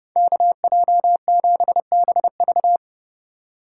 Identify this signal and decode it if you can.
Morse code, heard as a keyed tone, about 29 wpm, 700 Hz. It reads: KJ7BV